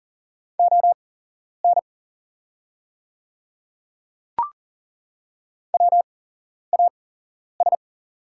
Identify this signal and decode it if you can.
Morse code, heard as a keyed tone, about 40 wpm, spaced as for 11 wpm, 700 Hz. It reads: ON WAS
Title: 8x Character Spacing - Top 100 Words - 40wpm